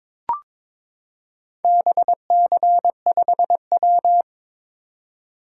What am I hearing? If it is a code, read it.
Morse code, heard as a keyed tone, about 22 wpm, 700 Hz. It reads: BC5W
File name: practice / 22wpm / Four Letter-Number Combinations - 22wpm